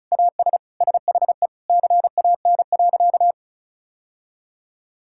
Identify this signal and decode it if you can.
Morse code, heard as a keyed tone, about 34 wpm, 700 Hz. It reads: AS SHE CAN.
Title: Sentences from Top 1000 Words - Review Part 5 - 35wpm